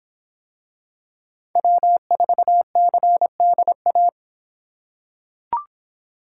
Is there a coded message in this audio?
Yes: Morse code from 1.6 to 4.1 s, heard as a keyed tone, about 24 wpm, 700 Hz. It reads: W4CDA